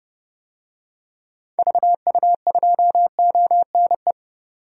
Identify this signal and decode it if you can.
Morse code, heard as a keyed tone, about 31 wpm, 700 Hz. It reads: VU2ONE
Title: DXCC-100 Call Signs - 6 Characters - 30wpm